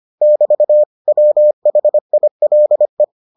Morse code, heard as a keyed tone, about 25 wpm, 600 Hz. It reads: <BT> WHILE